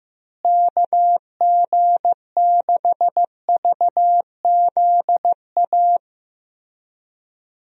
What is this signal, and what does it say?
Morse code, heard as a keyed tone, about 15 wpm, 700 Hz. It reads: KG6VZA
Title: USA Call Signs - 6 Characters - Repeated 3x - 15wpm